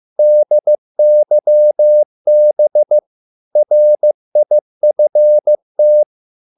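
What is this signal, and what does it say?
Morse code, heard as a keyed tone, about 15 wpm, 600 Hz. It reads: DYB RIFT
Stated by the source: Morse code at a slow 15 wpm from CW med Gnister.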